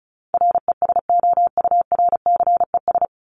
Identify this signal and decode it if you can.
Morse code, heard as a keyed tone, about 34 wpm, 700 Hz. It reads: RESOURCES
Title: ES:S Suffix - Top 100 Words with Base Words 35wpm